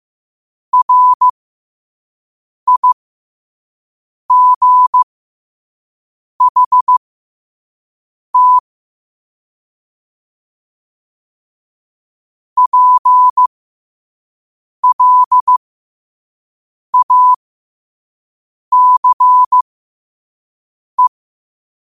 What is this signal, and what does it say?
Morse code, heard as a keyed tone, about 15 wpm, spaced as for 5 wpm, 1000 Hz. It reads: RIGHT PLACE
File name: qotdfiles / 5wpm